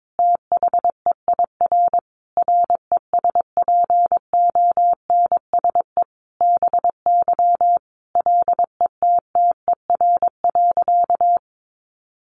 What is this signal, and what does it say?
Morse code, heard as a keyed tone, about 22 wpm, 700 Hz. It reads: THEIR RESPONSE BY LETTER.
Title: Sentences from Top 1400 Words 22wpm